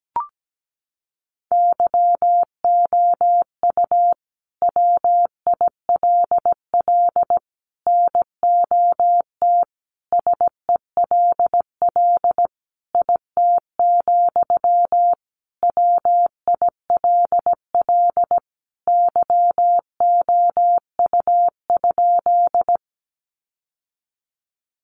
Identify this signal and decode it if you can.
Morse code, heard as a keyed tone, about 17 wpm, 700 Hz. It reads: YOU WILL NOT SELL IT, WILL YOU?
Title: Sentences from Top 1100 Words 17wpm